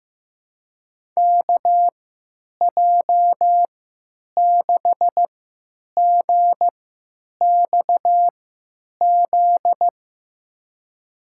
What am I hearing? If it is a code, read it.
Morse code, heard as a keyed tone, about 15 wpm, spaced as for 9 wpm, 700 Hz. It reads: KJ6GXZ
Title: USA Call Signs - 3x Character Spacing - 15wpm